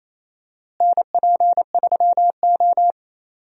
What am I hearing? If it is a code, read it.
Morse code, heard as a keyed tone, about 28 wpm, 700 Hz. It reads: NP3O
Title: USA Call Signs - 4 Characters - Repeated 2x - 28wpm